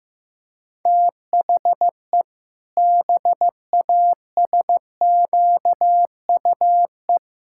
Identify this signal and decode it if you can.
Morse code, heard as a keyed tone, about 15 wpm, 700 Hz. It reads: THE BASQUE
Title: Pangram Challenge v2 - 15wpm